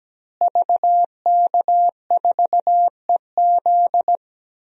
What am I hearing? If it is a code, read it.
Morse code, heard as a keyed tone, about 17 wpm, 700 Hz. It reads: VK4EZ
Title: DXCC-100 Call Signs - 5 Characters - Repeated 3x - 17wpm